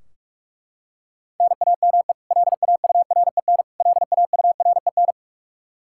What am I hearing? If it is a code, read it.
Morse code, heard as a keyed tone, about 41 wpm, 700 Hz. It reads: NAME LAUREN LAUREN
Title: Practice QSO - Exchange 1 - Beginning - 45wpm